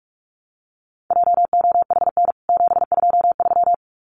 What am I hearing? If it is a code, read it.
Morse code, heard as a keyed tone, about 44 wpm, 700 Hz. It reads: JOHN 723
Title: CWT Exchange 45wpm